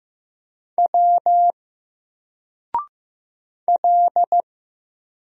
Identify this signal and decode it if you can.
Morse code, heard as a keyed tone, about 15 wpm, 700 Hz. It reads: W L
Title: Single Character (Letter:Number:Punctuation:Prosign) 15wpm